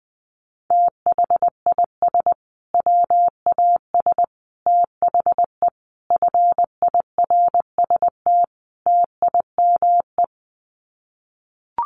Morse code, heard as a keyed tone, about 20 wpm, 700 Hz. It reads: THIS WAS THE FIRST TIME
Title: Sets of 5 Words Encore 20wpm